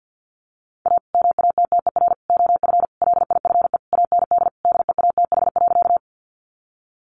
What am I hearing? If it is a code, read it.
Morse code, heard as a keyed tone, about 50 wpm, 700 Hz. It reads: A MATTER OF LIFE AND DEATH.
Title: Sentences from Top 1100 Words 50wpm